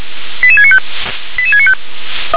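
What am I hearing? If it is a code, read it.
Five-tone selective call (ZVEI), heard as quick runs of five tones, tones of about 70 ms. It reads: 88695 88695
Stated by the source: Received on an SDR as ZVEI five-tone call, address 88776.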